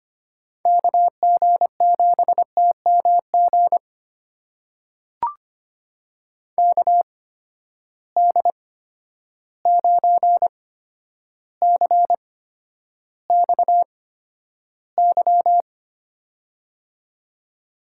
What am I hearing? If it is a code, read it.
Morse code, heard as a keyed tone, about 25 wpm, spaced as for 11 wpm, 700 Hz. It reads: KG7TMG KD9CXY
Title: USA Call Signs - 8x Character Spacing - 25wpm